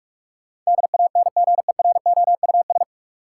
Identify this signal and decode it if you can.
Morse code, heard as a keyed tone, about 42 wpm, 700 Hz. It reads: DANGEROUS
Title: Sentences from Top 1200 Words - New Word Drill - 45wpm